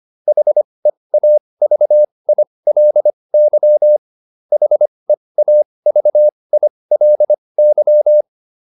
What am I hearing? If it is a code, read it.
Morse code, heard as a keyed tone, about 25 wpm, spaced as for 20 wpm, 600 Hz. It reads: HEAVILY HEAVILY